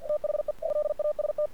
Morse code, heard as a keyed tone, about 46 wpm, 600 Hz. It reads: THE LAST
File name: archived / CRSnet